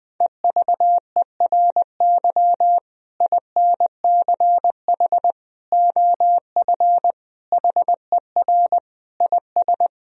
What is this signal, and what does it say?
Morse code, heard as a keyed tone, about 20 wpm, 700 Hz. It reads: EVERY INCH OF HER IS
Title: Sentences from Top 1000 Words - Review Part 3 - 20wpm